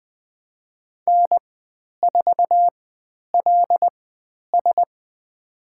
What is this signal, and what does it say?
Morse code, heard as a keyed tone, about 20 wpm, spaced as for 10 wpm, 700 Hz. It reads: N4LS
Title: 095 Lesson 16g - Letter c - Call Signs 20wpm